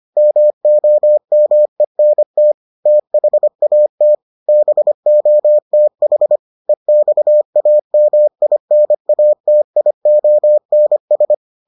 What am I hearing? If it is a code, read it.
Morse code, heard as a keyed tone, about 25 wpm, 600 Hz. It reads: MOMENT THAT BOTH EXAMINATIONS